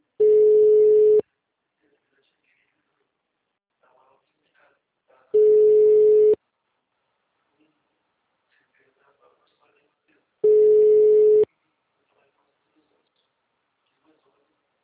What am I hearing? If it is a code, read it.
Ringback tone (Europe, 425 Hz), with speech.